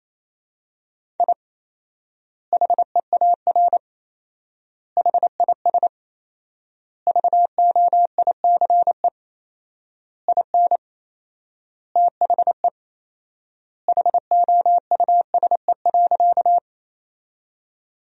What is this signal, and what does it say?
Morse code, heard as a keyed tone, about 28 wpm, 700 Hz. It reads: I HEAR HIS VOICE IN THE HOUSE.
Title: 4x Word-Spacing - Sentences from Top 500 Words - 28wpm